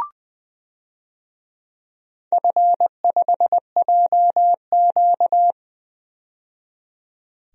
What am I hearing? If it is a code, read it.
Morse code, heard as a keyed tone, about 20 wpm, 700 Hz. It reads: F5JQ
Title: DXCC-100 Call Signs - 4 Characters - 20wpm